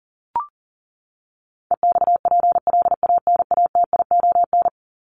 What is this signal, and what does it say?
Morse code, heard as a keyed tone, about 40 wpm, 700 Hz. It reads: EXPLANATION
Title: Top 1751-2000 Words 40wpm